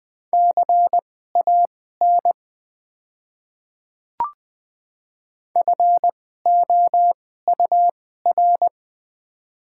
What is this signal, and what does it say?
Morse code, heard as a keyed tone, about 20 wpm, spaced as for 14 wpm, 700 Hz. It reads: CAN FOUR
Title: 2x Character Spacing - Top 300 Words - 20wpm